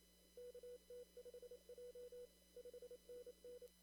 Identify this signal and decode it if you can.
Morse code, heard as a keyed tone, about 27 wpm, 500 Hz. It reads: KT5J 5NN
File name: CW / kt5j wpx cw-op-k5pi